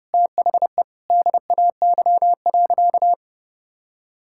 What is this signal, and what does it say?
Morse code, heard as a keyed tone, about 30 wpm, 700 Hz. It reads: THE DAY.
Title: Sentences from Top 1600 Words 30wpm